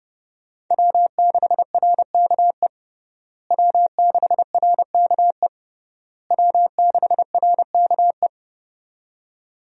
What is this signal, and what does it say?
Morse code, heard as a keyed tone, about 32 wpm, 700 Hz. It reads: W6RKE W6RKE W6RKE
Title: USA Call Signs - 5 Characters - Repeated 3x - 30wpm